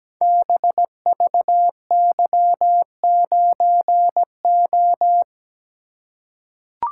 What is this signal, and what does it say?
Morse code, heard as a keyed tone, about 17 wpm, 700 Hz. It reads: BVY9O
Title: Code Groups - 5 Letter:Numbers - 17wpm